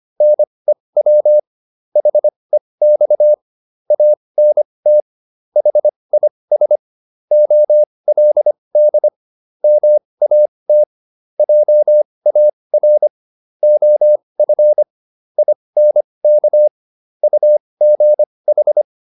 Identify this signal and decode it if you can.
Morse code, heard as a keyed tone, about 25 wpm, spaced as for 20 wpm, 600 Hz. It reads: NEW HEX ANT HIS OLD MAT JAR OF INK UGH